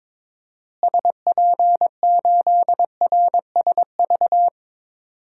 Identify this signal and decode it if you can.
Morse code, heard as a keyed tone, about 22 wpm, 700 Hz. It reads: SP8RSV
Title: DXCC-100 Call Signs - 6 Characters - Repeated 2x - 22wpm